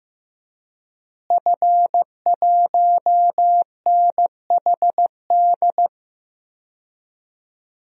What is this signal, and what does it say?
Morse code, heard as a keyed tone, about 15 wpm, 700 Hz. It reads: F1NHD
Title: DXCC-100 Call Signs - 5 Characters - Repeated 2x - 15wpm